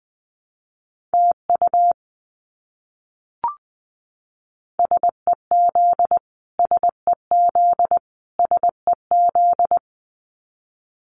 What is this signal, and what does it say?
Morse code, heard as a keyed tone, about 20 wpm, 700 Hz. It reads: TU SEZ SEZ SEZ
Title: Abbreviations Repeated 3x - 20wpm